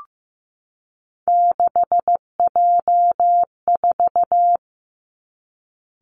Morse code, heard as a keyed tone, about 15 wpm, 700 Hz. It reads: 6J4
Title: Three Challenging Letter-Number Combinations 15wpm